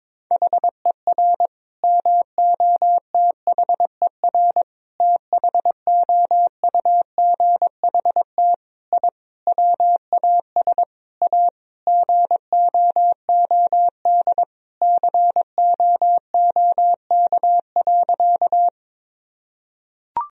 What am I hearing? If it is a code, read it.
Morse code, heard as a keyed tone, about 21 wpm, 700 Hz. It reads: HER MOTHER THOUGHT I WAS A GOOD COOK.
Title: Sentences from Top 1000 Words - Review Part 4 - 22wpm